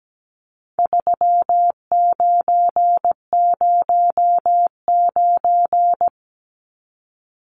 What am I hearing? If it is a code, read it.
Morse code, heard as a keyed tone, about 17 wpm, 700 Hz. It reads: 3909
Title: Numbers - 4-Digit 17wpm